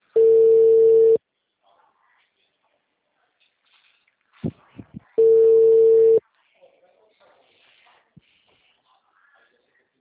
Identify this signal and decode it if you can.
Ringback tone (China, 450 Hz), with speech.